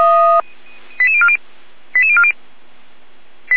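Five-tone selective call (ZVEI), heard as quick runs of five tones, tones of about 70 ms. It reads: 88040 88040
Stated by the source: ZVEI five-tone selective call received with an SDR.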